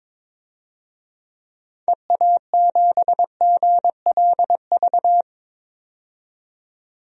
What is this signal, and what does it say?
Morse code, heard as a keyed tone, about 21 wpm, 700 Hz. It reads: EA7GLV
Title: DXCC-100 Call Signs - 6 Characters - 22wpm